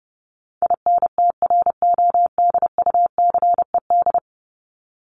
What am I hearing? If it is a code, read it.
Morse code, heard as a keyed tone, about 30 wpm, 700 Hz. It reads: INTRODUCED
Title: ED Suffix - Top 100 Words with Base Words 30wpm